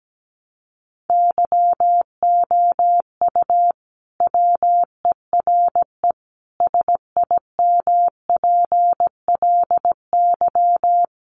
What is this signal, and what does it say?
Morse code, heard as a keyed tone, about 17 wpm, 700 Hz. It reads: YOU WERE SIMPLY
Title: Sentences from Top 1100 Words 17wpm